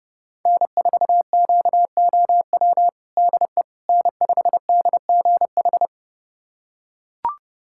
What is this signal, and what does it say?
Morse code, heard as a keyed tone, about 30 wpm, 700 Hz. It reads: N4QOW DE N5DGH
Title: USA Callsign DE Callsign - 5 Characters - 30wpm